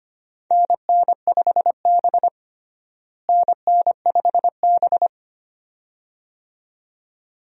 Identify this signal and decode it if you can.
Morse code, heard as a keyed tone, about 25 wpm, 700 Hz. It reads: NN5B NN5B
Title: USA Call Signs - 4 Characters - Repeated 2x - 25wpm